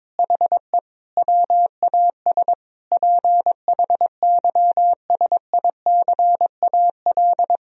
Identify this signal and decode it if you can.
Morse code, heard as a keyed tone, about 21 wpm, 700 Hz. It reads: HE WAS PHYSICAL
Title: Sentences from Top 1000 Words - Review Part 4 - 22wpm